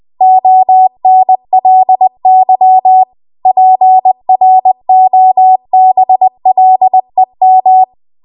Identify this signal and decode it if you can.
Morse code, heard as a keyed tone, about 20 wpm, 750 Hz. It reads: ONLY PROBLEM
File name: Archive / 20 WPM / 20WPM